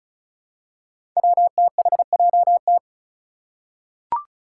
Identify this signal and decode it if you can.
Morse code, heard as a keyed tone, about 34 wpm, 700 Hz. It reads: WTHJT
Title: Five Challenging Letter-Number Combinations 35wpm